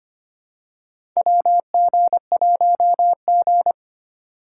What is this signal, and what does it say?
Morse code, heard as a keyed tone, about 25 wpm, 700 Hz. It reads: WG1G
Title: USA Call Signs - 4 Characters - 2x Character Spacing - 25wpm